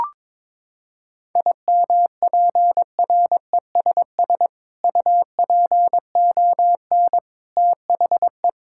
Morse code, heard as a keyed tone, about 22 wpm, 700 Hz. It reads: IMPRESS UPON THE
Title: Sentences from Top 1600 Words 22wpm